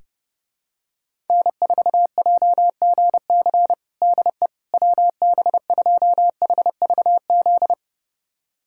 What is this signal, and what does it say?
Morse code, heard as a keyed tone, about 32 wpm, 700 Hz. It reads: N4JGC DE WB2HVZ
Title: USA Callsign DE Callsign 30wpm